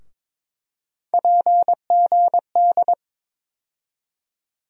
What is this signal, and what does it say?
Morse code, heard as a keyed tone, about 21 wpm, 700 Hz. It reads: PGD
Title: USA Call Sign Postfixes - 3 Letters 22wpm